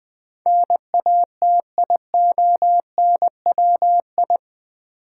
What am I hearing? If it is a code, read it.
Morse code, heard as a keyed tone, about 20 wpm, 700 Hz. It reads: NATIONWI
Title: Build A Word - 4 Letters At A Time - 10-Letter Words - 20wpm